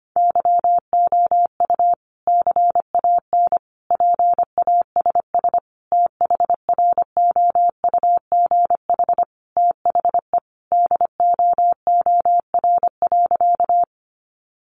Morse code, heard as a keyed tone, about 25 wpm, 700 Hz. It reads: YOU CAN PASS THROUGH THE DOOR.
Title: Sentences from Top 400 Words 25wpm